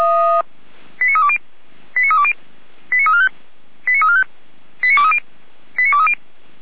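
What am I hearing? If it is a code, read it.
Five-tone selective call (ZVEI), heard as quick runs of five tones, tones of about 70 ms. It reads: 89320 89320 89345 89345 89230 89230
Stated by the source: ZVEI five-tone selective call 89025 received by SDR.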